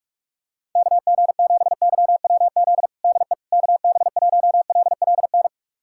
Morse code, heard as a keyed tone, about 46 wpm, 700 Hz. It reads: KG7YWZ DE KB1LLN
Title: USA Callsign DE Callsign - 6 Characters - 45wpm